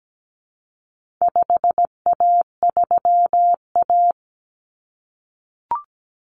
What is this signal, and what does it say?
Morse code, heard as a keyed tone, about 17 wpm, 700 Hz. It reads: 5A3A
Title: Common Cut Numbers - 4-Digit 17wpm